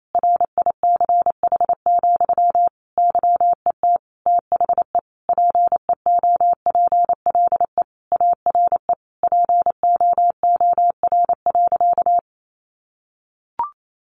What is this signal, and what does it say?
Morse code, heard as a keyed tone, about 28 wpm, 700 Hz. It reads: RICH, YET THE PEOPLE ARE POOR.
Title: Sentences from Top 1100 Words 28wpm